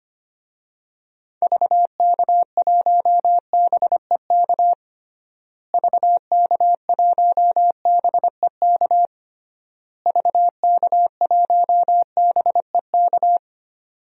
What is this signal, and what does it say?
Morse code, heard as a keyed tone, about 25 wpm, 700 Hz. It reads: VK1BEK VK1BEK VK1BEK
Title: DXCC-100 Call Signs - 6 Characters - Repeated 3x - 25wpm